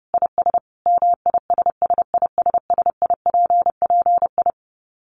Morse code, heard as a keyed tone, about 30 wpm, 700 Hz. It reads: IS MISSISSIPPI
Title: Dit Challenge 30wpm